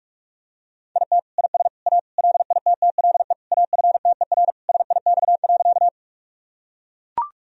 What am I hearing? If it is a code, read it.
Morse code, heard as a keyed tone, about 44 wpm, 700 Hz. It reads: IT IS A LITTLE AFTER SIX.
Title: Sentences from Top 1000 Words - Review Part 2 - 45wpm